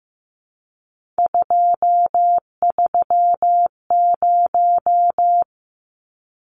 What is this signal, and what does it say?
Morse code, heard as a keyed tone, about 15 wpm, 700 Hz. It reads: 230
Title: Numbers - 3-Digit 15wpm